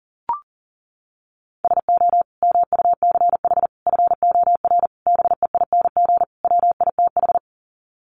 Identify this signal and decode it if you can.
Morse code, heard as a keyed tone, about 40 wpm, 700 Hz. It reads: SO MUCH FOR BEING WITH
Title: Sets of 5 Words Encore 40wpm